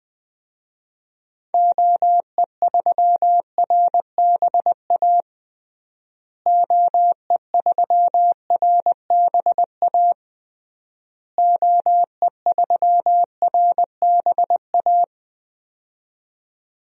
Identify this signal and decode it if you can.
Morse code, heard as a keyed tone, about 21 wpm, 700 Hz. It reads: OE3RBA OE3RBA OE3RBA
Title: DXCC-100 Call Signs - 6 Characters - Repeated 3x - 20wpm